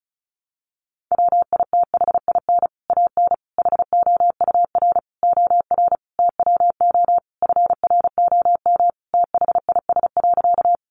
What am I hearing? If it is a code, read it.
Morse code, heard as a keyed tone, about 34 wpm, 700 Hz. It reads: WITHIN AN HOUR OR TWO FROM THIS.
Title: Sentences from Top 700 Words 35wpm